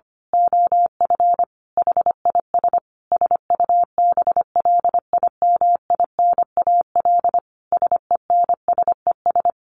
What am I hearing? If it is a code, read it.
Morse code, heard as a keyed tone, about 25 wpm, 700 Hz. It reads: OF HIS SUBLIMINAL SENSES
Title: A Sound Of Thunder 25wpm